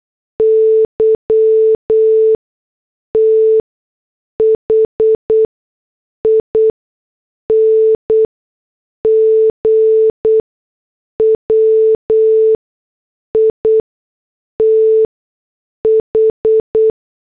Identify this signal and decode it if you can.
Morse code, heard as a keyed tone, about 8 wpm, spaced as for 6 wpm, 440 Hz. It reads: YTHING WITH
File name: morse / headlines